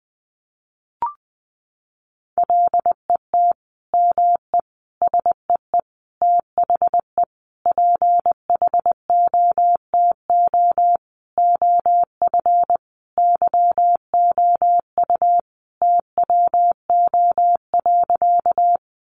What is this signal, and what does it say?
Morse code, heard as a keyed tone, about 20 wpm, 700 Hz. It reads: LET ME SEE THE PHOTO OF YOU TWO.
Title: Sentences from Top 1200 Words 20wpm